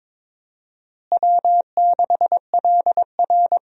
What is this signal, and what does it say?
Morse code, heard as a keyed tone, about 22 wpm, 700 Hz. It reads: W6LR
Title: USA Call Signs - 4 Characters - Repeated 3x - 22wpm